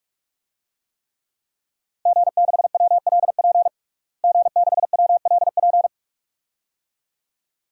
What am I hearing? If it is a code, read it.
Morse code, heard as a keyed tone, about 42 wpm, 700 Hz. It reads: G6WLP G6WLP
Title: DXCC-100 Call Signs - 5 Characters - Repeated 2x - 45wpm